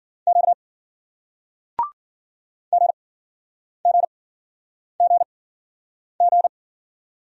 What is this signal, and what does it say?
Morse code, heard as a keyed tone, about 50 wpm, spaced as for 9 wpm, 700 Hz. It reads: Q GGGG
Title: Speed-Racing Single Letters 40wpm